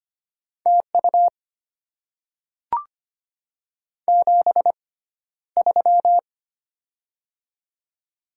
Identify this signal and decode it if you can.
Morse code, heard as a keyed tone, about 25 wpm, spaced as for 13 wpm, 700 Hz. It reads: TU 73
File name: Top 20 QSO Elements - 6x Character Spacing - 25wpm